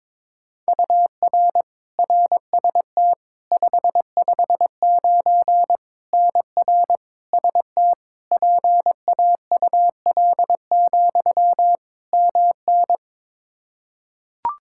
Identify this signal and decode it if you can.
Morse code, heard as a keyed tone, about 22 wpm, 700 Hz. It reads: UR RST 559 NR ST PAUL, MN
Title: Practice QSO - Exchange 1 - Beginning - 22wpm